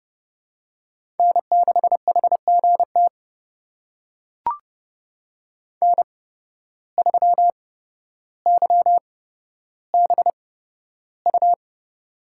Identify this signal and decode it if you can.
Morse code, heard as a keyed tone, about 30 wpm, spaced as for 13 wpm, 700 Hz. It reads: N6HGT N3YBU
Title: USA Call Signs - 5 Characters - 8x Character Spacing - 30wpm